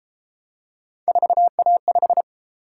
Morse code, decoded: 4A5

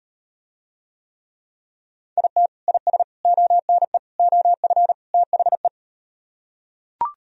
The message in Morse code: IT IS ONE OF THE

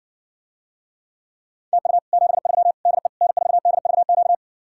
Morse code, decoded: TU 73 DE N4DVQ

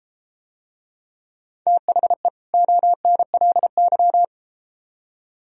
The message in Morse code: THE ONLY

